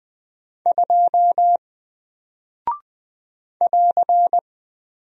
Morse code, heard as a keyed tone, about 20 wpm, 700 Hz. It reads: 2 <AR>
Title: Single Character (Letter:Number:Punctuation:Prosign) Alt 20wpm